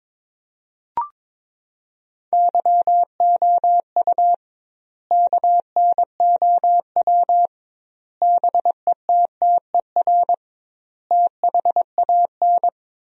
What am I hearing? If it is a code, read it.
Morse code, heard as a keyed tone, about 22 wpm, 700 Hz. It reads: YOU KNOW BETTER THAN
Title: 2x Word-Spacing - Sentences from Top 400 Words - 22wpm